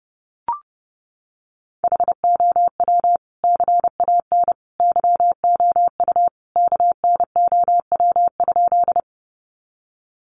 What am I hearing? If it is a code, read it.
Morse code, heard as a keyed tone, about 30 wpm, 700 Hz. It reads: HOW CAN YOU KNOW?
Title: Sentences from Top 100 Words 30wpm